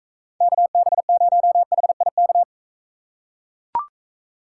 Morse code, heard as a keyed tone, about 42 wpm, 700 Hz. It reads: KB0HIK